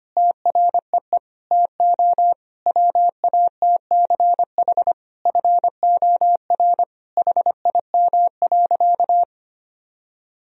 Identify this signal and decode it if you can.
Morse code, heard as a keyed tone, about 25 wpm, 700 Hz. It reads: TREE TO WATCH FOR HIM.
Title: Sentences from Top 1000 Words - Review Part 1 - 25wpm